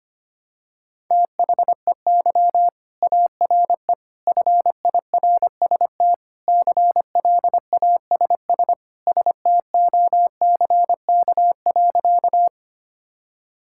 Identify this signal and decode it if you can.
Morse code, heard as a keyed tone, about 25 wpm, 700 Hz. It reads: THEY ARE FIRST CLASS STOCK.
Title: Sentences from Top 1000 Words - Review Part 5 - 25wpm